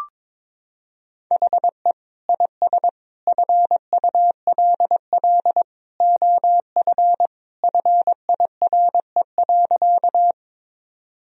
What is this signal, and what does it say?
Morse code, heard as a keyed tone, about 22 wpm, 700 Hz. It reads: HE IS FULL OF FIRE.
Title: Sentences from Top 1000 Words - Review Part 3 - 22wpm